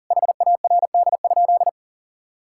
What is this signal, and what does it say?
Morse code, heard as a keyed tone, about 40 wpm, 700 Hz. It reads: HARD?
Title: Sentences from Top 1000 Words - Review Part 5 - 40wpm